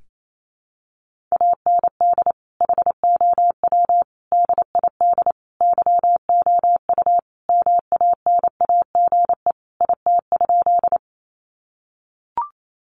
Morse code, heard as a keyed tone, about 28 wpm, 700 Hz. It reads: AND HOW DID YOU MANAGE IT?
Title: Sentences from Top 1400 Words 28wpm